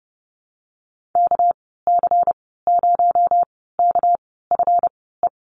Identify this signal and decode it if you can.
Morse code, heard as a keyed tone, about 30 wpm, spaced as for 17 wpm, 700 Hz. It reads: KC0KFE
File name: Lesson 37g - Letter x - Call Signs 30wpm